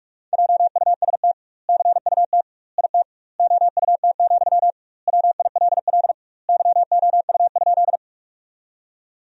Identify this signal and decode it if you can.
Morse code, heard as a keyed tone, about 42 wpm, 700 Hz. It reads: JUST CUT IT OUT, WILL YOU?
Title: 2x Word-Spacing - Sentences from Top 300 Words - 45wpm